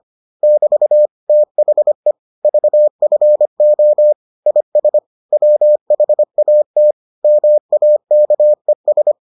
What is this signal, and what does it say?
Morse code, heard as a keyed tone, about 25 wpm, 600 Hz. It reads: <BT> THE VFO IS WHAT MAKES